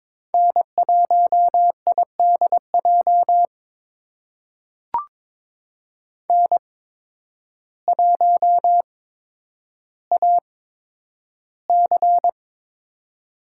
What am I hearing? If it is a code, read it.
Morse code, heard as a keyed tone, about 22 wpm, spaced as for 10 wpm, 700 Hz. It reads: N1IDJ N1AC
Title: USA Call Signs - 8x Character Spacing - 22wpm